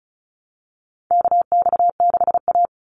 Morse code, heard as a keyed tone, about 34 wpm, 700 Hz. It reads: KX6A
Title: NA SOTA Chaser Callsigns 35wpm